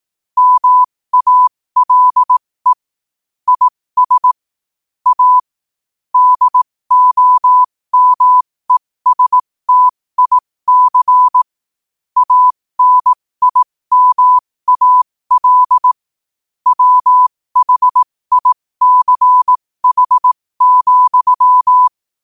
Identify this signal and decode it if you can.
Morse code, heard as a keyed tone, about 18 wpm, spaced as for 15 wpm, 1000 Hz. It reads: MALE IS A DOMESTIC ANIMAL WHICH,
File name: qotdfiles / 15wpm